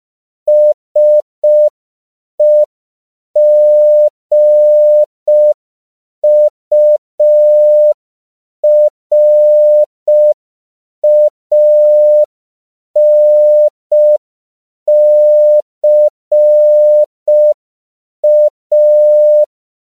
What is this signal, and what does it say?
Morse code, heard as a keyed tone, about 5 wpm, 600 Hz. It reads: SEGURANCA
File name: CW / 5PPM